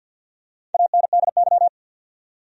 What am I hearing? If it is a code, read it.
Morse code, heard as a keyed tone, about 51 wpm, 700 Hz. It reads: ANDY
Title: USA Common Names - Max 4 Letters - 50wpm